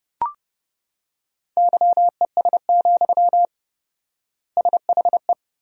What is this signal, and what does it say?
Morse code, heard as a keyed tone, about 30 wpm, 700 Hz. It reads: YES, SHE
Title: 4x Word-Spacing - Sentences from Top 500 Words - 30wpm